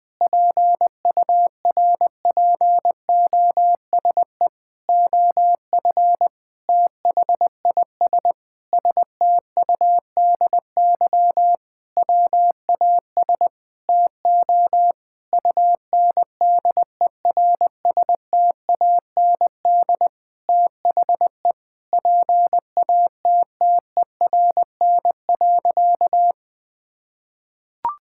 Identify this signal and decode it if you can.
Morse code, heard as a keyed tone, about 21 wpm, 700 Hz. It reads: PURPOSE OF THIS STUDY WAS TO UNDERSTAND THE PATTERN.
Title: Sentences with 5-Word Phrases - Spoken Beginning - 20wpm